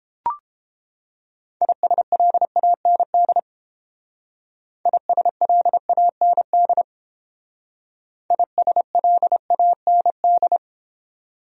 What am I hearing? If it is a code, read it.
Morse code, heard as a keyed tone, about 28 wpm, 700 Hz. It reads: ISLAND ISLAND ISLAND